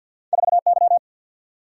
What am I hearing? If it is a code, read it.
Morse code, heard as a keyed tone, about 53 wpm, 700 Hz. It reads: 4Y
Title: Single-Letter Single-Number 50wpm